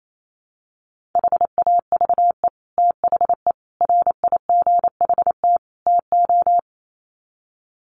Morse code, heard as a keyed tone, about 28 wpm, 700 Hz. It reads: HAVE THE RIGHT TO
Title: Sets of 4 Words 28wpm